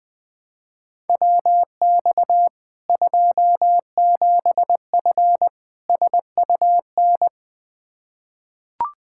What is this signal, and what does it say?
Morse code, heard as a keyed tone, about 21 wpm, 700 Hz. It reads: WX 27F SUN
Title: Practice QSO - Exchange 2 - Info - 20wpm